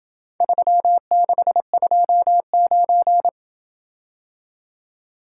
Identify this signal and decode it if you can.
Morse code, heard as a keyed tone, about 27 wpm, 700 Hz. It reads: 3629